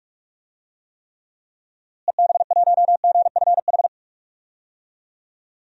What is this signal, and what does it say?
Morse code, heard as a keyed tone, about 44 wpm, 700 Hz. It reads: EB1GFH
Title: DXCC-100 Call Signs - 6 Characters - Repeated 3x - 45wpm